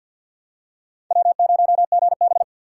Morse code, heard as a keyed tone, about 51 wpm, 700 Hz. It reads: W0GB
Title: USA Call Signs - 3x Character Spacing - 50wpm